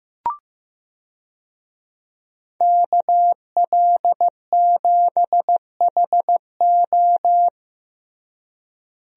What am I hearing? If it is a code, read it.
Morse code, heard as a keyed tone, about 16 wpm, 700 Hz. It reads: KL7HO
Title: DXCC-100 Call Signs - 5 Characters - Repeated 2x - 15wpm